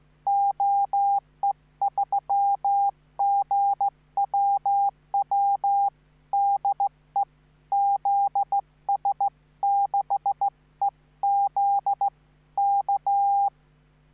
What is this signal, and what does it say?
Morse code, heard as a keyed tone, about 15 wpm, 800 Hz. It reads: OE3GWW DE ZS6EZ K